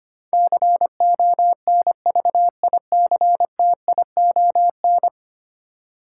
Morse code, heard as a keyed tone, about 25 wpm, 700 Hz. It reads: CONVICTION
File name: Ten Letter Words - 25wpm